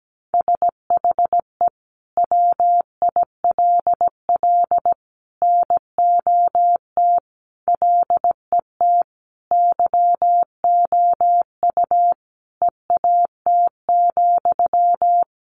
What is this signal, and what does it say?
Morse code, heard as a keyed tone, about 17 wpm, 700 Hz. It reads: SHE WILL NOT LET YOU EAT,
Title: Sentences from Top 1500 Words 17wpm